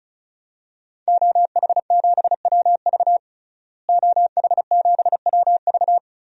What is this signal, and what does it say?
Morse code, heard as a keyed tone, about 34 wpm, 700 Hz. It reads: OH7WV OH7WV
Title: DXCC-100 Call Signs - 5 Characters - Repeated 3x - 35wpm